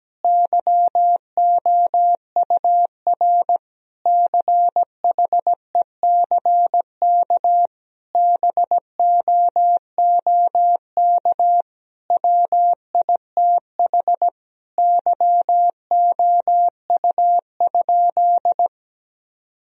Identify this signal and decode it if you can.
Morse code, heard as a keyed tone, about 17 wpm, 700 Hz. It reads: YOUR CHECK BOOK WITH YOU?
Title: Sentences from Top 1000 Words - Review Part 3 - 17wpm